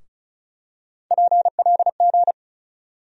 Morse code, heard as a keyed tone, about 36 wpm, 700 Hz. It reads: PLG
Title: USA Call Sign Postfixes - 3 Letters 35wpm